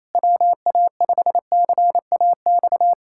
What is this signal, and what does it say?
Morse code, heard as a keyed tone, about 27 wpm, 700 Hz. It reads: WA5CAX
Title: USA Callsign DE Callsign - 6 Characters - 28wpm